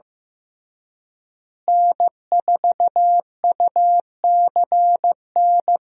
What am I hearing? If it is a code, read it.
Morse code, heard as a keyed tone, about 15 wpm, 700 Hz. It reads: N4UCN
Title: USA Call Signs - 5 Characters - Repeated 3x - 15wpm